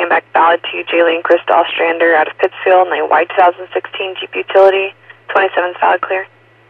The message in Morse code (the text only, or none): none